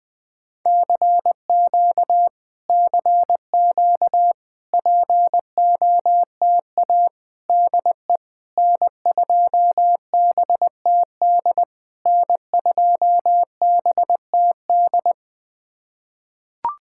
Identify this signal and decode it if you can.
Morse code, heard as a keyed tone, about 20 wpm, 700 Hz. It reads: CQ CQ POTA DE N2BTD N2BTD
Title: POTA QSO Practice - 20wpm